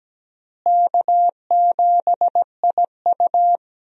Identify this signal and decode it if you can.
Morse code, heard as a keyed tone, about 17 wpm, 700 Hz. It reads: K7IU
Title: USA Call Signs - 4 Characters - Repeated 2x - 17wpm